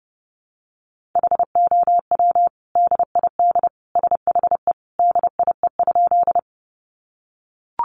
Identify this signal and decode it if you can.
Morse code, heard as a keyed tone, about 30 wpm, 700 Hz. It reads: HOW DID SHE DIE?
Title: Sentences from Top 1300 Words 30wpm